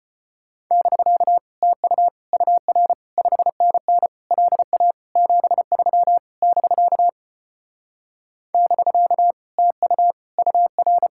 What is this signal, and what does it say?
Morse code, heard as a keyed tone, about 33 wpm, 700 Hz. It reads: <BK> TU UR 5NN LA 73 <BK> <BK> TU UR